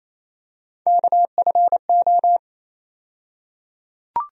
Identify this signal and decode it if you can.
Morse code, heard as a keyed tone, about 28 wpm, 700 Hz. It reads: KFO
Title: USA Call Sign Postfixes - 3 Letters 28wpm